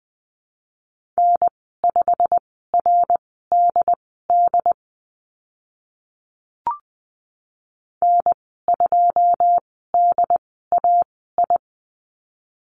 Morse code, heard as a keyed tone, about 20 wpm, spaced as for 15 wpm, 700 Hz. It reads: N5RDD N2DAI
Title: 088 Lesson 15g - Number 5 - Call Signs 20wpm